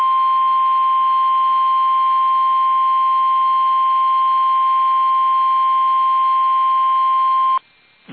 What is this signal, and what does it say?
Emergency Alert System (SAME) tones (the 1050 Hz attention tone). It is heard from 1.2 to 7.3 s.
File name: eas / KUSO / KUSO